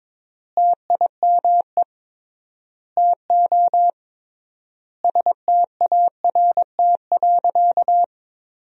Morse code, heard as a keyed tone, about 22 wpm, 700 Hz. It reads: TIME TO START.
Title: 3x Word-Spacing - Sentences from Top 300 Words - 22wpm